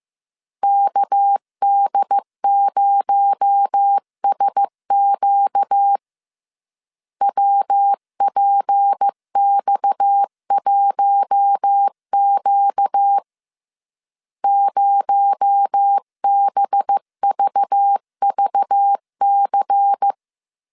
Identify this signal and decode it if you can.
Morse code, heard as a keyed tone, about 15 wpm, 800 Hz. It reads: KD0SQ WPX1Q 0BVVC
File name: CW / DET / 1 6 1500C S B 14wpm 2958